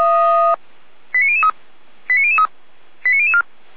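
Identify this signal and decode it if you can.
Five-tone selective call (ZVEI), heard as quick runs of five tones, tones of about 70 ms. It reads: 89003 89003 89004
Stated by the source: ZVEI five-tone selective call 89010, received by SDR, logged on the Feuerwehr Orlamünde fire brigade server.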